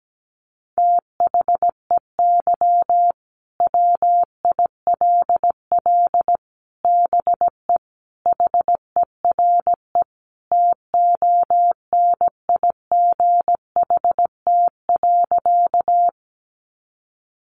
Morse code, heard as a keyed tone, about 17 wpm, 700 Hz. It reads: THEY WILL BE HERE TONIGHT.
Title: Sentences from Top 600 Words 17wpm